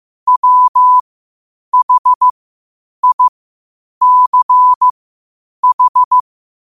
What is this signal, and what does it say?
Morse code, heard as a keyed tone, about 15 wpm, spaced as for 8 wpm, 1000 Hz. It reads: WHICH
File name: qotdfiles / 8wpm